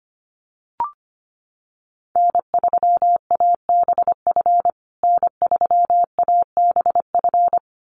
Morse code, heard as a keyed tone, about 25 wpm, 700 Hz. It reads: N3ABF N3ABF
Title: USA Call Signs Repeated 2x 25wpm